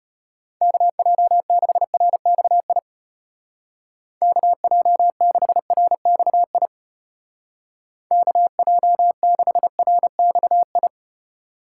Morse code, heard as a keyed tone, about 31 wpm, 700 Hz. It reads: KJ6RXI KJ6RXI KJ6RXI
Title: Speed Racing - USA 2x3 Call Signs 25wpm